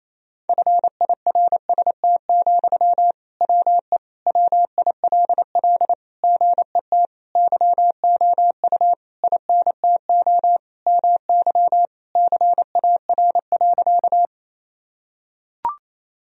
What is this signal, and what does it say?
Morse code, heard as a keyed tone, about 28 wpm, 700 Hz. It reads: FIRST, WE WILL GET YOU INTO MY CAR.
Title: Sentences from Top 1000 Words - Review Part 1 - 28wpm